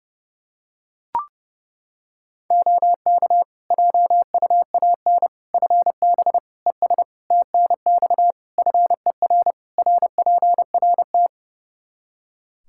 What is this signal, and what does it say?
Morse code, heard as a keyed tone, about 32 wpm, 700 Hz. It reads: OK JUAN FB ES TNX FER RPRT
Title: K1USN SST QSO Practice 30wpm